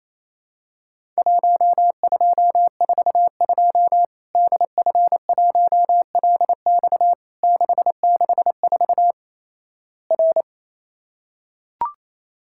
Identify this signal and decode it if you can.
Morse code, heard as a keyed tone, about 27 wpm, 700 Hz. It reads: 1242 DF1LX 664 R
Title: Worked All Europe DX Contest - QTC Exchange - 28wpm